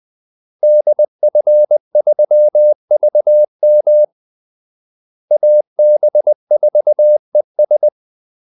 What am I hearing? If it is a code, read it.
Morse code, heard as a keyed tone, about 20 wpm, 600 Hz. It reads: DF3VM AB4ES
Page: Kaldesignaler 20 wpm